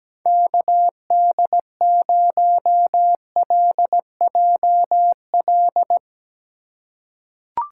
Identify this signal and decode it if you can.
Morse code, heard as a keyed tone, about 17 wpm, 700 Hz. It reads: KD0LJL
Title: USA Call Signs - 4x Character Spacing - 17wpm